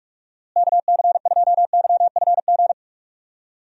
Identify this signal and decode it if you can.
Morse code, heard as a keyed tone, about 43 wpm, 700 Hz. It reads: KC2YFG
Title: USA Call Signs - 6 Characters - Repeated 2x - 45wpm